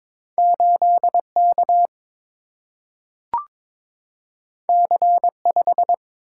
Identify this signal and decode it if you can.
Morse code, heard as a keyed tone, about 21 wpm, 700 Hz. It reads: 8 K C 5
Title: Single-Letter Single-Number 22wpm